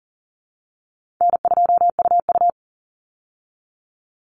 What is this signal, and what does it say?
Morse code, heard as a keyed tone, about 40 wpm, 700 Hz. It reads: N2UU
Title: Call Signs Repeated 2x - 40wpm